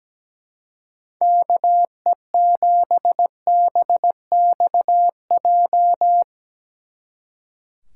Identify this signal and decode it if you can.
Morse code, heard as a keyed tone, about 18 wpm, 700 Hz. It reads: KE7BXJ
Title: Speed Racing - USA Call Signs 17wpm